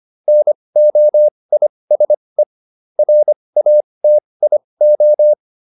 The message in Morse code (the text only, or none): NOISE RATIO